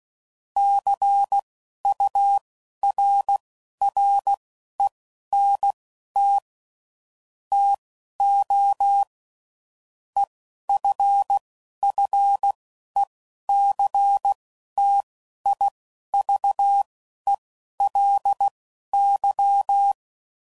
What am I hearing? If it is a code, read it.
Morse code, heard as a keyed tone, about 16 wpm, spaced as for 11 wpm, 800 Hz. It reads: CURRENT TO EFFECTIVELY
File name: CW Code Groups @10WPM / Technical Subjects / XC